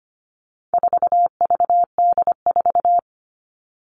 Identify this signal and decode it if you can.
Morse code, heard as a keyed tone, about 25 wpm, 700 Hz. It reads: 4VD4
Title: All Cut Numbers - 4-Digit 25wpm